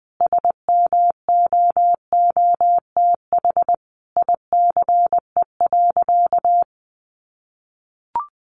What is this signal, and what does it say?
Morse code, heard as a keyed tone, about 20 wpm, 700 Hz. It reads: SMOOTH ICE.
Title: Sentences from Top 1400 Words 20wpm